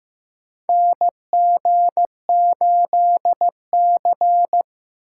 Morse code, heard as a keyed tone, about 15 wpm, 700 Hz. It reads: NG8C
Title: USA Call Signs - 4 Characters - 2x Character Spacing - 15wpm